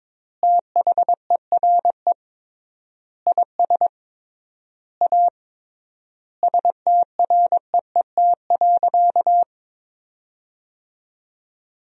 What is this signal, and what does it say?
Morse code, heard as a keyed tone, about 22 wpm, 700 Hz. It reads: THERE IS A STREET.
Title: 3x Word-Spacing - Sentences from Top 500 Words - 22wpm